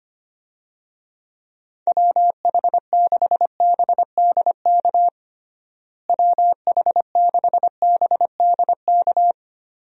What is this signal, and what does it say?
Morse code, heard as a keyed tone, about 25 wpm, 700 Hz. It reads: WH6BDK WH6BDK
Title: USA Call Signs - 6 Characters - Repeated 2x - 25wpm